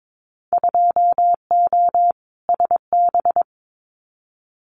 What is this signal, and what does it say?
Morse code, heard as a keyed tone, about 22 wpm, 700 Hz. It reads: 2O SB
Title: Winter Field Day Exchanges 22wpm